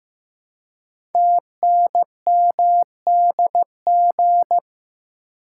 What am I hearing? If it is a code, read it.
Morse code, heard as a keyed tone, about 15 wpm, 700 Hz. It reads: TNMDG
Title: Code Groups - 5 Letters - 15wpm